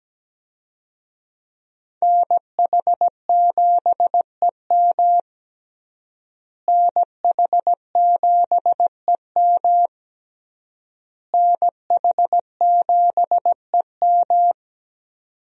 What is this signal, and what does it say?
Morse code, heard as a keyed tone, about 17 wpm, 700 Hz. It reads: NH7EM NH7EM NH7EM
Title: USA Call Signs - 5 Characters - Repeated 3x - 17wpm